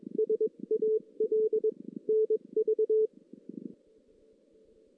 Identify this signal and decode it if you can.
Morse code, heard as a keyed tone, about 21 wpm, spaced as for 19 wpm, 440 Hz. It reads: SAL NV
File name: 2016 JAN CW NAQP / 2016 Jan naqp k5ot - 0-3